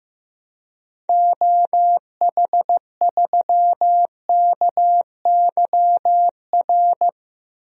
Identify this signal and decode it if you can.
Morse code, heard as a keyed tone, about 15 wpm, 700 Hz. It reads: OH3KYR
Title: DXCC-100 Call Signs - 6 Characters - 15wpm